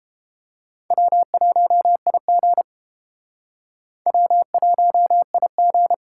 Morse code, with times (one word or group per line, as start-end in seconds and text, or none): 0.9-2.6 W1IG
4.1-5.9 W1IG